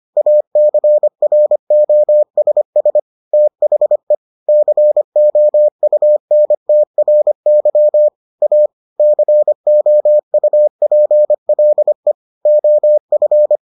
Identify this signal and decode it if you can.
Morse code, heard as a keyed tone, about 25 wpm, 600 Hz. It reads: ACROSS THE COUNTRY A COUPLE OF